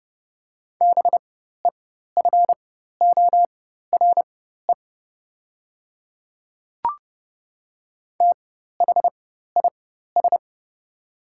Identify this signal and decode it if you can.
Morse code, heard as a keyed tone, about 30 wpm, spaced as for 14 wpm, 700 Hz. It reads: BEFORE THIS
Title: 4x Character Spacing - Top 500 Words - 30wpm